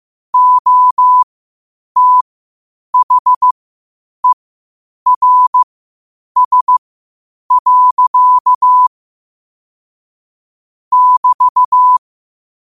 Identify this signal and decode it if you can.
Morse code, heard as a keyed tone, about 15 wpm, spaced as for 8 wpm, 1000 Hz. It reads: OTHERS. <BT>
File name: qotdfiles / 8wpm